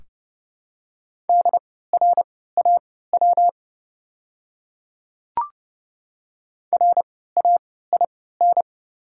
Morse code, heard as a keyed tone, about 31 wpm, spaced as for 17 wpm, 700 Hz. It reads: DRAW RAIN
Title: 3x Character Spacing - Top 300 Words - 30wpm